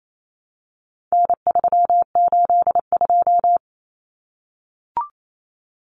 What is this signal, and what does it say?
Morse code, heard as a keyed tone, about 28 wpm, 700 Hz. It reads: N382
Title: Common Cut Numbers - 4-Digit 28wpm